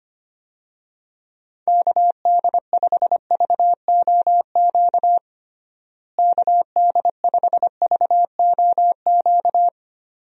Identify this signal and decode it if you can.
Morse code, heard as a keyed tone, about 25 wpm, 700 Hz. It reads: KD5VOQ KD5VOQ
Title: USA Call Signs - 6 Characters - Repeated 2x - 25wpm